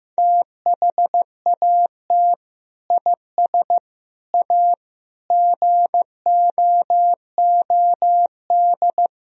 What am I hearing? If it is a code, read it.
Morse code, heard as a keyed tone, about 16 wpm, 700 Hz. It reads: THAT IS A GOOD
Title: Sentences from Top 1000 Words - Review Part 4 - 15wpm